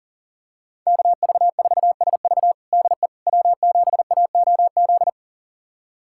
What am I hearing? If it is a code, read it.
Morse code, heard as a keyed tone, about 40 wpm, 700 Hz. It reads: KV4SV DE W7AOZ
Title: USA Callsign DE Callsign - 5 Characters - 40wpm